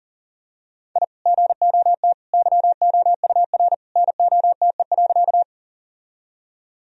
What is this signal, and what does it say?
Morse code, heard as a keyed tone, about 40 wpm, 700 Hz. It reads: I GOT YOUR NOTE.
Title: Sentences from Top 1000 Words - Review Part 2 - 40wpm